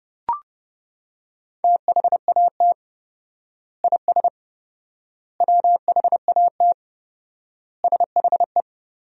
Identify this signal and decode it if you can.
Morse code, heard as a keyed tone, about 32 wpm, 700 Hz. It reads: THAT IS WHAT SHE
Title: 4x Word-Spacing - Sentences from Top 100 Words - 30wpm